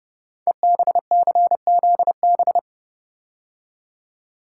Morse code, heard as a keyed tone, about 29 wpm, 700 Hz. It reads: EBCZB
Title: Code Groups - 5 Letters - 30wpm